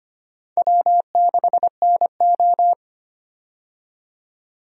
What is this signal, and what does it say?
Morse code, heard as a keyed tone, about 25 wpm, 700 Hz. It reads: W6NO
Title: USA Call Signs - 4 Characters - 25wpm